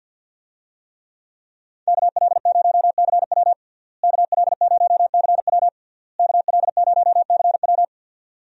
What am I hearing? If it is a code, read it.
Morse code, heard as a keyed tone, about 47 wpm, 690 Hz. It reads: KL0CW KL0CW KL0CW
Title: USA Call Signs - 5 Characters - Repeated 3x - 50wpm